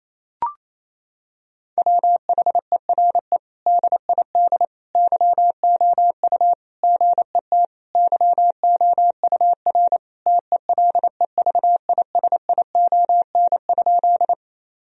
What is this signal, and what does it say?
Morse code, heard as a keyed tone, about 27 wpm, 700 Hz. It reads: WHERE DID YOU GET YOUR TELEVISION?
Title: Sentences from Top 1000 Words - Review Part 4 - 28wpm